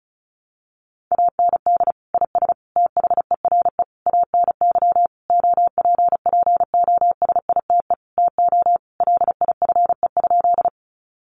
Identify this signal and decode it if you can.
Morse code, heard as a keyed tone, about 34 wpm, 700 Hz. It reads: AND IS THERE ANY OPPOSITE TO LIFE?
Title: Sentences from Top 1300 Words 35wpm